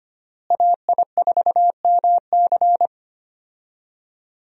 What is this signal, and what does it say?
Morse code, heard as a keyed tone, about 25 wpm, 700 Hz. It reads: AI4MC
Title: USA Call Signs - 5 Characters - Repeated 3x - 25wpm